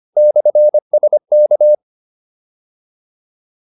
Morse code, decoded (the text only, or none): /SK